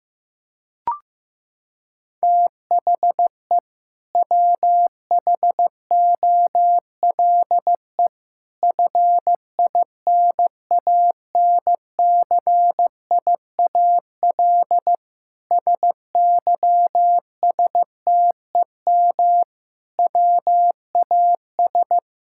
Morse code, heard as a keyed tone, about 15 wpm, 700 Hz. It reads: THE WHOLE FINANCIAL SYSTEM WAS
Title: Sentences from Top 1000 Words - Review Part 5 - 15wpm